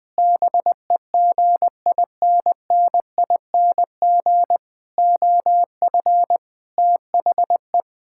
Morse code, heard as a keyed tone, about 21 wpm, 700 Hz. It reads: BEGINNING OF THE
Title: Sentences with 5-Word Phrases - Spoken Beginning - 20wpm